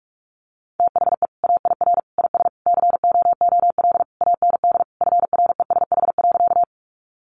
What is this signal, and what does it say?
Morse code, heard as a keyed tone, about 46 wpm, 700 Hz. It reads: THE AIR IS COOL AND FRESH.
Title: Sentences from Top 1200 Words 45wpm